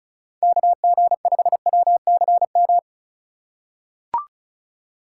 Morse code, heard as a keyed tone, about 34 wpm, 700 Hz. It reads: KG5WCM
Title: USA Call Signs - 2x Character Spacing - 35wpm